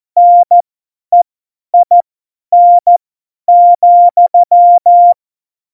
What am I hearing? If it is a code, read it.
Morse code, heard as a keyed tone, about 14 wpm, spaced as for 10 wpm, 700 Hz. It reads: NEIN,
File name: Morsekurs / Witze / Witz 13. Februar 2018, 50 F70